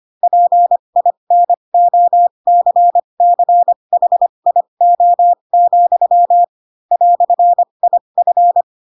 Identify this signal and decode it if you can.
Morse code, heard as a keyed tone, about 25 wpm, spaced as for 22 wpm, 700 Hz. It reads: PINOCCHIO, "IF
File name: Pinocchio / Pinocchio ch7 18 25 ik1hge